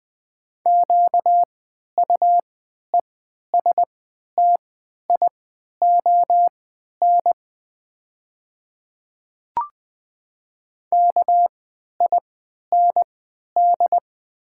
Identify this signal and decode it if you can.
Morse code, heard as a keyed tone, about 20 wpm, spaced as for 11 wpm, 700 Hz. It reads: QUESTION KIND
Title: 3x Character Spacing - Top 500 Words - 20wpm